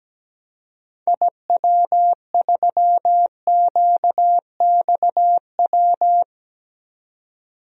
Morse code, heard as a keyed tone, about 17 wpm, 700 Hz. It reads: IW3QXW
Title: DXCC-100 Call Signs - 6 Characters - 17wpm